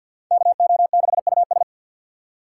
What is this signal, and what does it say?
Morse code, heard as a keyed tone, about 53 wpm, 700 Hz. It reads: KO6US
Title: USA Call Signs - 5 Characters - 3x Character Spacing - 50wpm